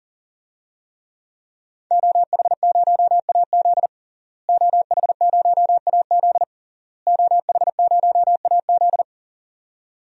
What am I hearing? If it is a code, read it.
Morse code, heard as a keyed tone, about 37 wpm, 700 Hz. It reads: OH0AZ OH0AZ OH0AZ
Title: DXCC-100 Call Signs - 5 Characters - Repeated 3x - 40wpm